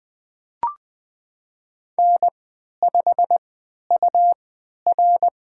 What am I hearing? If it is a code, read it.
Morse code, heard as a keyed tone, about 20 wpm, spaced as for 11 wpm, 700 Hz. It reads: N5UR
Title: USA Call Signs - 4 Characters - 3x Character Spacing - 20wpm